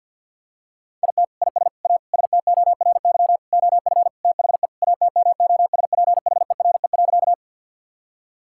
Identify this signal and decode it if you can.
Morse code, heard as a keyed tone, about 48 wpm, 700 Hz. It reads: IT IS A STORY OF THE ATMOSPHERE.
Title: Sentences from Top 1600 Words 50wpm